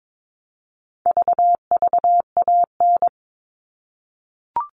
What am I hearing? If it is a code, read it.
Morse code, heard as a keyed tone, about 22 wpm, 700 Hz. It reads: VVAN
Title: All Cut Numbers - 4-Digit 22wpm